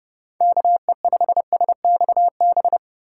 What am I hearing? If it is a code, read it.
Morse code, heard as a keyed tone, about 30 wpm, 700 Hz. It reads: KE5SXB
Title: USA Call Signs - 6 Characters - 4x Character Spacing - 30wpm